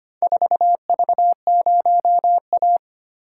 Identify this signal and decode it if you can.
Morse code, heard as a keyed tone, about 25 wpm, 700 Hz. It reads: 4V0A
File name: practice / 25wpm / Four Letter-Number Combinations - 25wpm